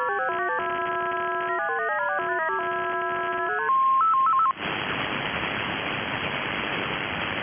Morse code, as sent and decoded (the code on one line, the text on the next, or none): none